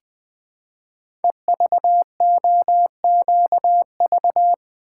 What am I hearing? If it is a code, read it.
Morse code, heard as a keyed tone, about 20 wpm, 700 Hz. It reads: EVOQV
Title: Code Groups - 5 Letters - 20wpm